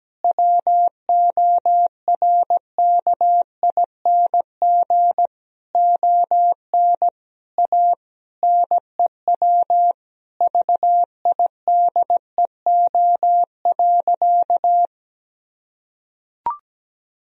Morse code, heard as a keyed tone, about 17 wpm, 700 Hz. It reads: WORKING ON A NEW VIDEO.
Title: Sentences from Top 1000 Words - Review Part 4 - 17wpm